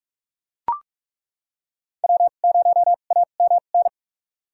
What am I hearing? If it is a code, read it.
Morse code, heard as a keyed tone, about 42 wpm, spaced as for 31 wpm, 700 Hz. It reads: W0AMN
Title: USA Call Signs - 5 Characters - 2x Character Spacing - 45wpm